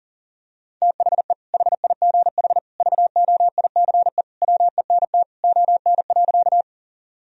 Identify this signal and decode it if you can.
Morse code, heard as a keyed tone, about 40 wpm, 700 Hz. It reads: THE HIGH VOICE WENT ON.
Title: Sentences from Top 1000 Words - Review Part 1 - 40wpm